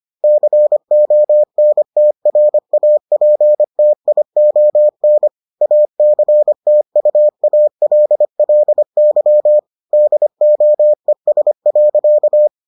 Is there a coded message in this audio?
Yes: Morse code from 0.2 to 12.5 s, heard as a keyed tone, about 25 wpm, 600 Hz. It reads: CONTRAPTION ACTUALLY DOES.